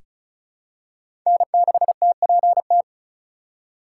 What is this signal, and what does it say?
Morse code, heard as a keyed tone, about 36 wpm, 700 Hz. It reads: N6TPT
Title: USA Call Signs - 8x Character Spacing - 35wpm